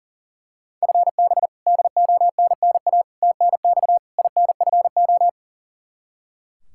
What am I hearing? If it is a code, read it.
Morse code, heard as a keyed tone, about 40 wpm, 700 Hz. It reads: FB DONNA TNX INFO
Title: Practice QSO - Exchange 3 - More Info - 40wpm